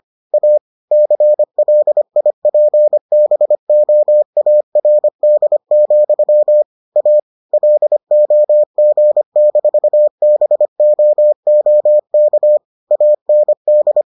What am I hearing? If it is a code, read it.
Morse code, heard as a keyed tone, about 25 wpm, 600 Hz. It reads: A CLIPBOARD, A LOG-BOOK AND